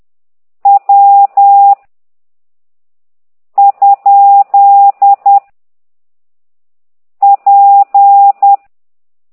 Morse code, heard as a keyed tone, about 10 wpm, 800 Hz. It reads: W ? P
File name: ct2idw / exerc9